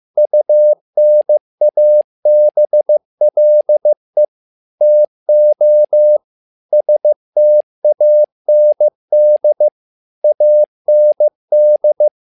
Morse code, heard as a keyed tone, about 15 wpm, 600 Hz. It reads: UNABLE TO STAND AND
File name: learning center / tower trouble / tower trouble 15wpm